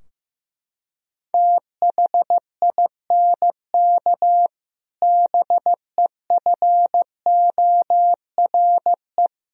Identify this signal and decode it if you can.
Morse code, heard as a keyed tone, about 15 wpm, 700 Hz. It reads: THINK BEFORE
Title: Sentences from Top 1000 Words - Review Part 3 - 15wpm